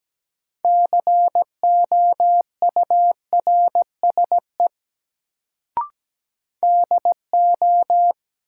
Morse code, heard as a keyed tone, about 17 wpm, 700 Hz. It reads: COURSE DO
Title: Sets of 2 Words - New Word Drill - 17wpm